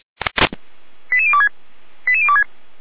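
Five-tone selective call (ZVEI), heard as quick runs of five tones, tones of about 70 ms. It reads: 88026 88026
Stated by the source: ZVEI five-tone call received with an SDR.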